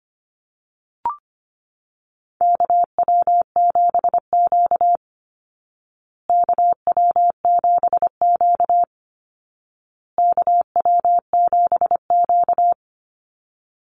Morse code, heard as a keyed tone, about 25 wpm, 700 Hz. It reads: KW7Q KW7Q KW7Q
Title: Call Signs Repeated 2x - 25wpm